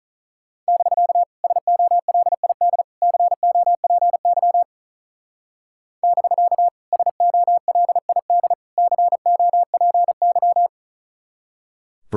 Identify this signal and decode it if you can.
Morse code, heard as a keyed tone, about 43 wpm, 700 Hz. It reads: <BK> SOLID COPY <BK> SOLID COPY
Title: Speed-Racing Common QSO Phrases Encore - 35wpm